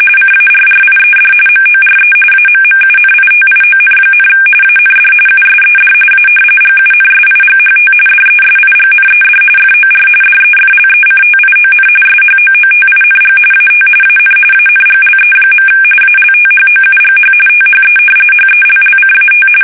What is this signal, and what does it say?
RTTY, about 45.45 baud. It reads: CARRIED MORE AMMUNITION, AND USED A GUNSIGHT DESIGNED FOR DEFLECTION SHOOTING AT AN ANGLE. ALTHOUGH THE HELLCAT
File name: fsk850